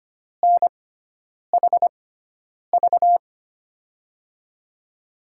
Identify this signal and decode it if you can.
Morse code, heard as a keyed tone, about 25 wpm, spaced as for 9 wpm, 700 Hz. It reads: NHV
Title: USA Call Signs - 6 Characters - 6x Character Spacing - 25wpm